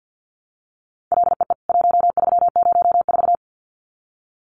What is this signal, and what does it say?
Morse code, heard as a keyed tone, about 50 wpm, 700 Hz. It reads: LEE 1304
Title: CWT Exchange 50wpm